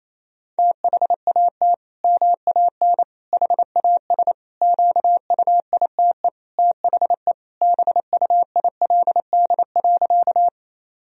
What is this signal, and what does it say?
Morse code, heard as a keyed tone, about 27 wpm, 700 Hz. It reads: THAT MAN HAS QUITE THE BUILD.
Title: Sentences from Top 1000 Words - Review Part 1 - 28wpm